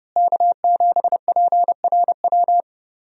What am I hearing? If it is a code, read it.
Morse code, heard as a keyed tone, about 30 wpm, 700 Hz. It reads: K7PRW
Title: USA Call Signs - 5 Characters - Repeated 2x - 30wpm